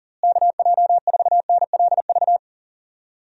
Morse code, heard as a keyed tone, about 40 wpm, 700 Hz. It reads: KJ4NLV
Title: USA Call Signs - 6 Characters - Repeated 2x - 40wpm